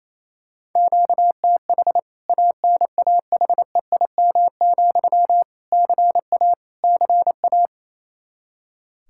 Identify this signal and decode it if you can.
Morse code, heard as a keyed tone, about 27 wpm, 700 Hz. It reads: QTH ANAHEIM, CA CA
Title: K1USN SST QSO Practice 28wpm